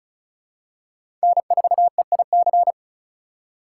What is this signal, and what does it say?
Morse code, heard as a keyed tone, about 34 wpm, 700 Hz. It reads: N4EIC
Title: USA Call Signs - 5 Characters - 3x Character Spacing - 35wpm